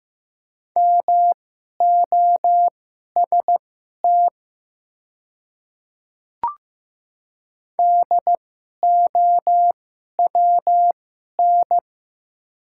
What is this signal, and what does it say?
Morse code, heard as a keyed tone, about 15 wpm, spaced as for 11 wpm, 690 Hz. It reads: MOST DOWN
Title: 2x Character Spacing - Top 100 Words - 15wpm